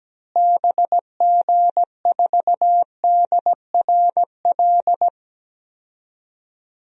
Morse code, heard as a keyed tone, about 17 wpm, 700 Hz. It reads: BG4DRL
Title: DXCC-100 Call Signs - 6 Characters - Repeated 3x - 17wpm